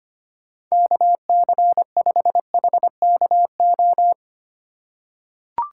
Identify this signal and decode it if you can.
Morse code, heard as a keyed tone, about 25 wpm, 700 Hz. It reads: KC5HKO
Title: USA Call Signs - 6 Characters - 4x Character Spacing - 25wpm